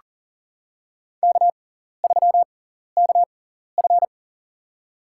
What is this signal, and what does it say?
Morse code, heard as a keyed tone, about 40 wpm, spaced as for 14 wpm, 700 Hz. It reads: K3KF
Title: USA Call Signs - 6x Character Spacing - 40wpm